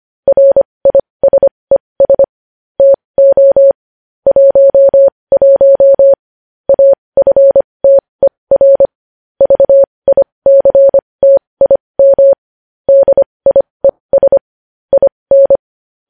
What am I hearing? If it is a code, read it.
Morse code, heard as a keyed tone, about 25 wpm, spaced as for 20 wpm, 550 Hz. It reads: RISES TO 11 AFTER VICTIM DIES IN
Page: news-headlines-in-morse-code-at-20-wpm